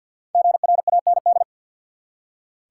Morse code, heard as a keyed tone, about 46 wpm, 700 Hz. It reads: GRAND
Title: Five Letter Words - 50wpm